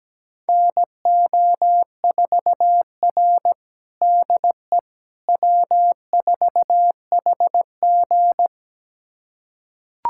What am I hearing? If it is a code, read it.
Morse code, heard as a keyed tone, about 17 wpm, 700 Hz. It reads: NO4R DE W4HG
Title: USA Callsign DE Callsign - 4 Characters - 17wpm